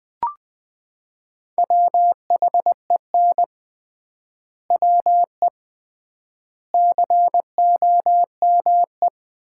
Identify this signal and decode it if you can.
Morse code, heard as a keyed tone, about 20 wpm, 700 Hz. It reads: WHEN WE COME
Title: 3x Word-Spacing - Sets of 4 Words - 20wpm